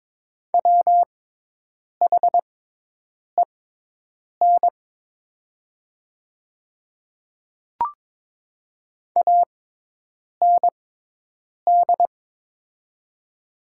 Morse code, heard as a keyed tone, about 22 wpm, spaced as for 8 wpm, 700 Hz. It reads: WHEN AND
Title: 6x Character Spacing - Top 100 Words - 22wpm